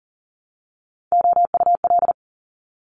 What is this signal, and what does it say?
Morse code, heard as a keyed tone, about 38 wpm, 700 Hz. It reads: OUL
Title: Top Three Letter Combinations in Words 40wpm